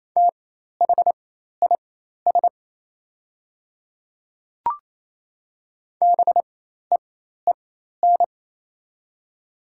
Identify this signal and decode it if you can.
Morse code, heard as a keyed tone, about 27 wpm, spaced as for 13 wpm, 700 Hz. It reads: THIS BEEN
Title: 4x Character Spacing - Top 100 Words - 28wpm